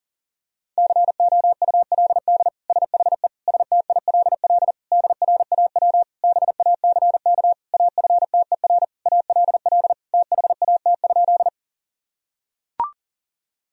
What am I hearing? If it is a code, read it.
Morse code, heard as a keyed tone, about 40 wpm, 700 Hz. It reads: COULD SHE STILL DRAW BACK AFTER ALL THAT?
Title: Sentences from Top 1000 Words - Review Part 1 - 40wpm